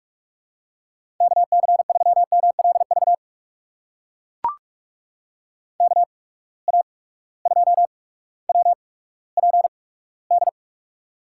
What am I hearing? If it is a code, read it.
Morse code, heard as a keyed tone, about 44 wpm, spaced as for 19 wpm, 700 Hz. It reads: KC3MLV KA2WPD
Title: USA Call Signs - 6 Characters - 8x Character Spacing - 45wpm